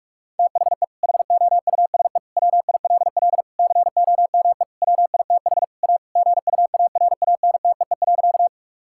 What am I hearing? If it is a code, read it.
Morse code, heard as a keyed tone, about 46 wpm, 700 Hz. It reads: THE HOUSE WILL COME WITH A GUARANTEE.
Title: Sentences from Top 1600 Words 45wpm